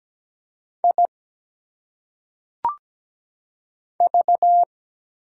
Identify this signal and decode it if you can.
Morse code, heard as a keyed tone, about 17 wpm, 700 Hz. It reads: I V
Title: Single Letters - Phonetic Pronunciation 17wpm